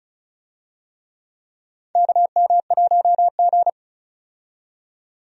Morse code, heard as a keyed tone, about 36 wpm, 700 Hz. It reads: KM1G
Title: USA Call Signs - 4 Characters - 35wpm